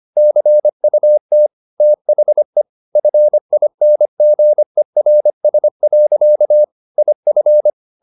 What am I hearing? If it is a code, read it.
Morse code, heard as a keyed tone, about 25 wpm, 600 Hz. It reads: CUT THE FINGERS. IF